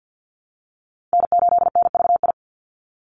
Morse code, heard as a keyed tone, about 50 wpm, 700 Hz. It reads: N8NVI
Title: USA Call Signs 50wpm